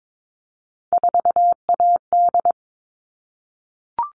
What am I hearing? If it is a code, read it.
Morse code, heard as a keyed tone, about 22 wpm, 700 Hz. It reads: 4AD
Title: All Cut Numbers - 3-Digit 22wpm